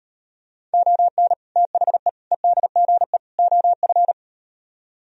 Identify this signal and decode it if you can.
Morse code, heard as a keyed tone, about 38 wpm, 700 Hz. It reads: ON THE EDGE OF